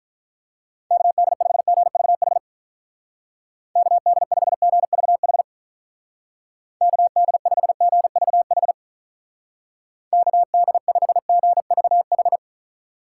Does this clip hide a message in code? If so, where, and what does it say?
Yes: Morse code from 0.9 to 12.4 s, heard as a keyed tone, about 46 wpm, 700 Hz. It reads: KD5GVH KD5GVH KD5GVH KD5GVH